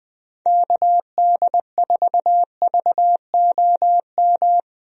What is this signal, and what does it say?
Morse code, heard as a keyed tone, about 20 wpm, 700 Hz. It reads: KD4VOM
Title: USA Call Signs - 6 Characters - Repeated 3x - 20wpm